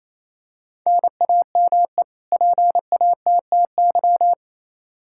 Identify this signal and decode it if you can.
Morse code, heard as a keyed tone, about 28 wpm, 700 Hz. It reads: NAME PATTY
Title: K1USN SST QSO Practice 28wpm